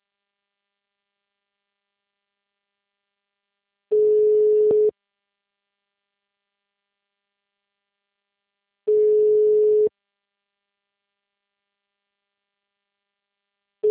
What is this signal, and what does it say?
Ringback tone (Europe, 425 Hz).